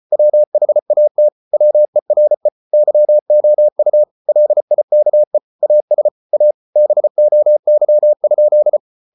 Morse code, heard as a keyed tone, about 34 wpm, 600 Hz. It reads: WHAT WERE YOU LIKE AS A BOY?